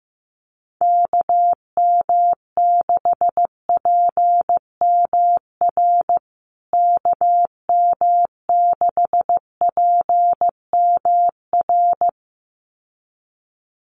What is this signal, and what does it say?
Morse code, heard as a keyed tone, about 15 wpm, 700 Hz. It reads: KM6PMR KM6PMR
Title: USA Call Signs Repeated 2x 15wpm